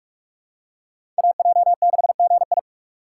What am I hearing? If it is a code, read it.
Morse code, heard as a keyed tone, about 46 wpm, 700 Hz. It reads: AJ6GI
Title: USA Call Signs - 5 Characters - 2x Character Spacing - 45wpm